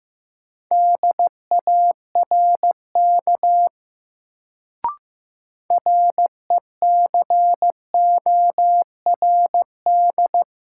Morse code, heard as a keyed tone, about 15 wpm, 700 Hz. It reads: DARK RECORD
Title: Sentences from Top 500 Words - New Word Drill - 15wpm